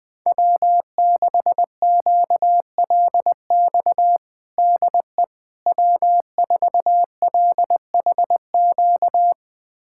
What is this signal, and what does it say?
Morse code, heard as a keyed tone, about 20 wpm, 700 Hz. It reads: W6QLX DE W4LHQ
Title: USA Callsign DE Callsign - 5 Characters - 20wpm